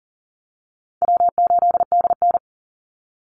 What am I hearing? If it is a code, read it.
Morse code, heard as a keyed tone, about 40 wpm, 700 Hz. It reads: W8DN
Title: Call Signs Repeated 2x - 40wpm